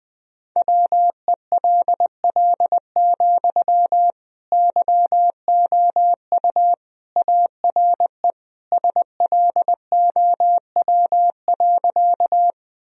Morse code, heard as a keyed tone, about 19 wpm, 700 Hz. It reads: WELL, YOU ARE SLOW.
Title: Sentences from Top 1000 Words - Review Part 2 - 20wpm